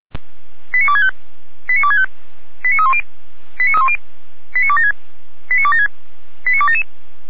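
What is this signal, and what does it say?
Five-tone selective call (ZVEI), heard as quick runs of five tones, tones of about 70 ms. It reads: 89265 89265 89310 89276 89276 89277
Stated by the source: ZVEI five-tone selective call 89108 received on an SDR by Feuerwehr Orlamuende.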